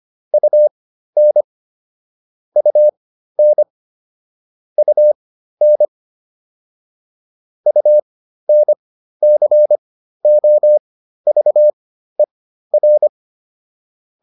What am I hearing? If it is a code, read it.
Morse code, heard as a keyed tone, about 25 wpm, spaced as for 13 wpm, 600 Hz. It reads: UN UN UN UNCOVER
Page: intermediate-practice-files